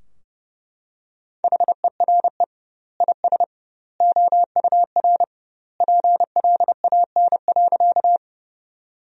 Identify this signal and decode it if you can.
Morse code, heard as a keyed tone, about 30 wpm, 700 Hz. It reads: HERE IS OUR PLAN.
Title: 2x Word-Spacing - Sentences from Top 500 Words - 30wpm